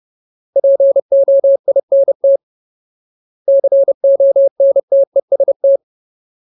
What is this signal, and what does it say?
Morse code, heard as a keyed tone, about 30 wpm, 550 Hz. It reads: POINT CONTEST
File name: Ultimate Word List / 30WPM Ultimate